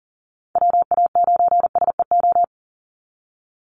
Morse code, heard as a keyed tone, about 40 wpm, 700 Hz. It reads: WA9SEO
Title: Call Signs Repeated 2x - 40wpm